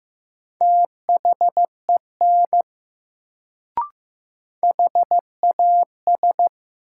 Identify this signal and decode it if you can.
Morse code, heard as a keyed tone, about 15 wpm, 700 Hz. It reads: THEN HAS
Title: Sentences from Top 100 Words - New Word Drill - 15wpm